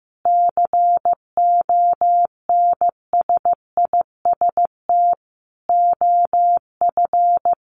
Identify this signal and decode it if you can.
Morse code, heard as a keyed tone, about 15 wpm, 700 Hz. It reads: CONSIST OF
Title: Sentences from Top 1400 Words 15wpm